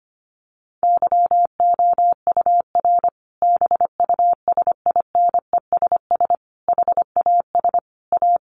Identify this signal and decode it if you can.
Morse code, heard as a keyed tone, about 25 wpm, 700 Hz. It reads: YOUR BUSINESS HAS A
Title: Sentences from Top 1200 Words 25wpm